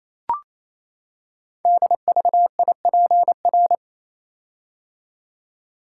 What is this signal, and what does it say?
Morse code, heard as a keyed tone, about 29 wpm, 700 Hz. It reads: DVIPR
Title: Code Groups - 5 Letters - 28wpm